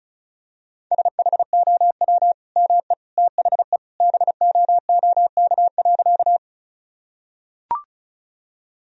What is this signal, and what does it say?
Morse code, heard as a keyed tone, about 36 wpm, 700 Hz. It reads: SHOW ME THE BOOK.
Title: Sentences from Top 1000 Words - Review Part 1 - 35wpm